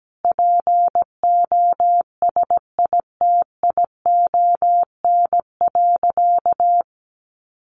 Morse code, heard as a keyed tone, about 17 wpm, 700 Hz. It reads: POSITION.
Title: Sentences from Top 1100 Words 17wpm